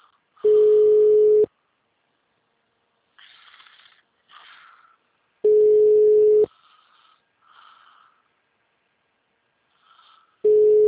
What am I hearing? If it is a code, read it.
Ringback tone (Europe, 425 Hz).
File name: out- -2007- - - .951